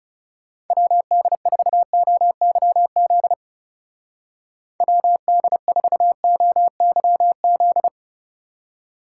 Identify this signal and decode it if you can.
Morse code, heard as a keyed tone, about 36 wpm, 700 Hz. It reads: WD4OYZ WD4OYZ
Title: Speed Racing - USA 2x3 Call Signs 30wpm